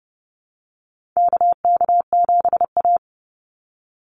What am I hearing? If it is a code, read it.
Morse code, heard as a keyed tone, about 30 wpm, 700 Hz. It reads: KK7A
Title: Call Signs Repeated 2x - 30wpm